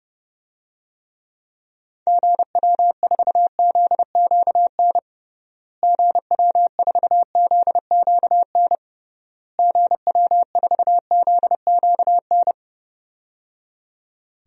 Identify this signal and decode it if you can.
Morse code, heard as a keyed tone, about 30 wpm, 700 Hz. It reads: GW4ZQN GW4ZQN GW4ZQN
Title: DXCC-100 Call Signs - 6 Characters - Repeated 3x - 30wpm